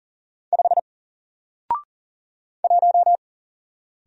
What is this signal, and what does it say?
Morse code, heard as a keyed tone, about 40 wpm, 700 Hz. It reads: 5 1
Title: Single Letter-Number 40wpm